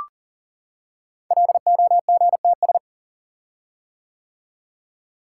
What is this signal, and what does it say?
Morse code, heard as a keyed tone, about 40 wpm, 690 Hz. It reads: LOGTS
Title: Code Groups - 5 Letter:Numbers - 40wpm